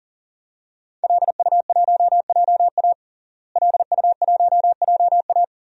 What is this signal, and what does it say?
Morse code, heard as a keyed tone, about 40 wpm, 700 Hz. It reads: LU1JA LU1JA
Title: DXCC-100 Call Signs - 5 Characters - Repeated 3x - 40wpm